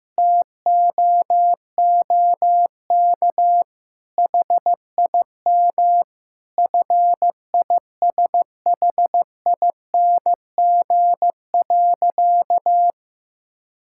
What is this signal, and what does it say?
Morse code, heard as a keyed tone, about 15 wpm, 700 Hz. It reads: TOOK HIM FISHING.
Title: Sentences from Top 1000 Words - Review Part 5 - 15wpm